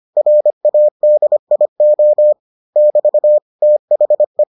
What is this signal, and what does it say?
Morse code, heard as a keyed tone, about 25 wpm, 600 Hz. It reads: RADIO <BT> THE